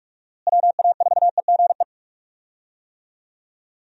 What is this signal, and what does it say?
Morse code, heard as a keyed tone, about 46 wpm, 700 Hz. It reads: WA4EGE
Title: USA Call Signs - 6 Characters - Repeated 3x - 45wpm